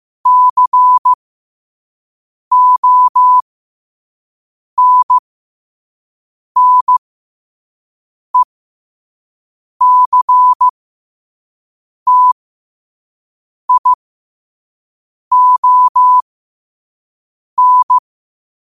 Morse code, heard as a keyed tone, about 15 wpm, spaced as for 5 wpm, 1000 Hz. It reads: CONNECTION
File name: qotdfiles / 5wpm